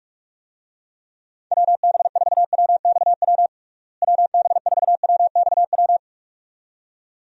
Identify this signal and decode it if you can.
Morse code, heard as a keyed tone, about 42 wpm, 700 Hz. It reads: WB4WXW WB4WXW
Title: USA Call Signs - 6 Characters - Repeated 2x - 45wpm